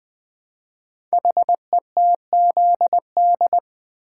Morse code, heard as a keyed tone, about 21 wpm, 700 Hz. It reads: HETZD
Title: Code Groups - 5 Letters - 20wpm